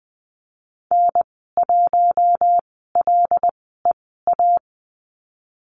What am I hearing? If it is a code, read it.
Morse code, heard as a keyed tone, about 20 wpm, spaced as for 15 wpm, 700 Hz. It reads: N1LEA
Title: 147 Lesson 24g - Letter y - Call Signs 20wpm